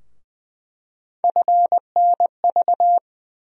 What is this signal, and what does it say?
Morse code, heard as a keyed tone, about 20 wpm, 700 Hz. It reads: FNV
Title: USA Call Sign Postfixes - 3 Letters 20wpm